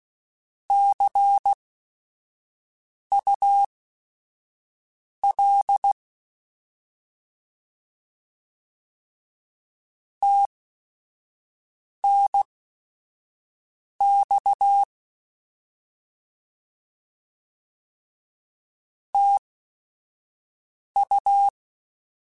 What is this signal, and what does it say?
Morse code, heard as a keyed tone, about 16 wpm, spaced as for 5 wpm, 800 Hz. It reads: CUL TNX TU
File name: Short Runs / Short Runs-mp3s / Words @5